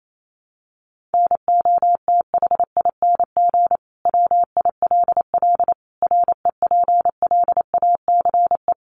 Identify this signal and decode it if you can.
Morse code, heard as a keyed tone, about 28 wpm, 700 Hz. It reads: NOTHING WILL REPLACE
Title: Sentences from Top 1400 Words 28wpm